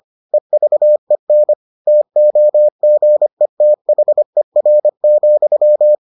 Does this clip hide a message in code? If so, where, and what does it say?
Yes: Morse code from 0.3 to 6.0 s, heard as a keyed tone, about 25 wpm, 600 Hz. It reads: EVEN TOGETHER,